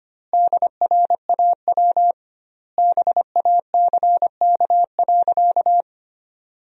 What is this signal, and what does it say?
Morse code, heard as a keyed tone, about 25 wpm, 700 Hz. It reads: DRAW BACK.
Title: 2x Word-Spacing - Sentences from Top 300 Words - 25wpm